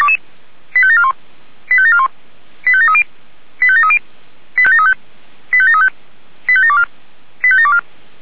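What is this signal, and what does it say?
Five-tone selective call (ZVEI), heard as quick runs of five tones, tones of about 70 ms. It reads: 85741 85741 85730 85730 85735 85735 85724 85724
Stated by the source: ZVEI five-tone selective call 85719 received on an SDR.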